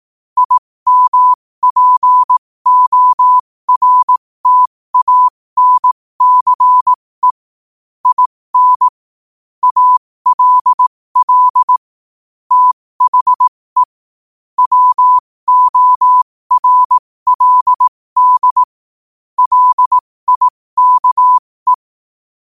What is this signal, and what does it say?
Morse code, heard as a keyed tone, about 18 wpm, spaced as for 15 wpm, 1000 Hz. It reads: IMPORTANCE IN ALL THE WORLD LIKE
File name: qotdfiles / 15wpm